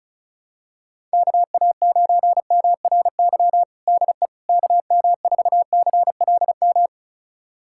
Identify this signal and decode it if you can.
Morse code, heard as a keyed tone, about 36 wpm, 700 Hz. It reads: KA9MRY DE KM4CLM
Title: USA Callsign DE Callsign - 6 Characters - 35wpm